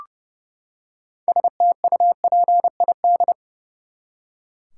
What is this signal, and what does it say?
Morse code, heard as a keyed tone, about 30 wpm, 700 Hz. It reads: STUPID
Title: Six Letter Words - 30wpm